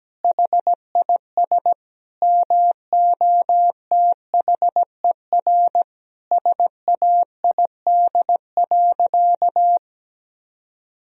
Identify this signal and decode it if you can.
Morse code, heard as a keyed tone, about 17 wpm, 700 Hz. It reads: HIS MOTHER SAID.
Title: Sentences from Top 1000 Words - Review Part 3 - 17wpm